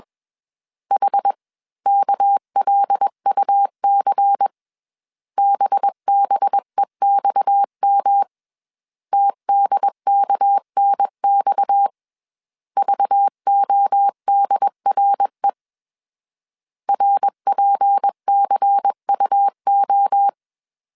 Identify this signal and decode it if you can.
Morse code, heard as a keyed tone, about 21 wpm, 800 Hz. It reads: H KLUC BBEXM TDKNX VODRE RPCUO
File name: CW / DET / 1 6 1500C S C 20wpm 2116